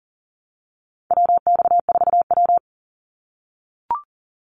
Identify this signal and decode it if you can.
Morse code, heard as a keyed tone, about 40 wpm, 700 Hz. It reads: WX4W
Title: Call Signs - 40wpm